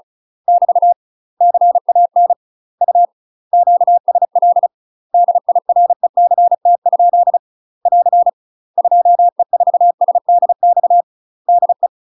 Morse code, heard as a keyed tone, about 36 wpm, 700 Hz. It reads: <BT> CAN U QSL DIRECT ? <AR> 2E4SDX DE
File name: UK MASTER QSOs TRAINING / 35WPM / UK MASTER QSOs 0009